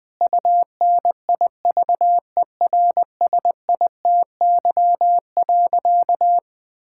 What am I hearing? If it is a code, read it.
Morse code, heard as a keyed tone, about 21 wpm, 700 Hz. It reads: UNIVERSITY.
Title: Sentences from Top 1000 Words - Review Part 5 - 20wpm